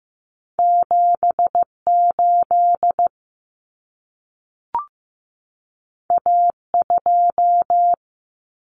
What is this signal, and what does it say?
Morse code, heard as a keyed tone, about 15 wpm, 700 Hz. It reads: 78 A2
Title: Common Cut Numbers - 2-Digit 15wpm